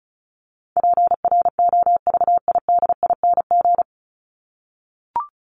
Morse code, heard as a keyed tone, about 34 wpm, 700 Hz. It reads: PROVIDING
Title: ING Suffix - Top 100 Words with Base Words 35wpm